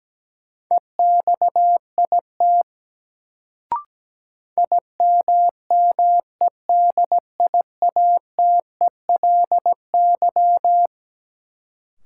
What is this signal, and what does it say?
Morse code, heard as a keyed tone, about 17 wpm, 700 Hz. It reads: EXIT IMMEDIATELY
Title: Sentences from Top 1100 Words - New Word Drill - 17wpm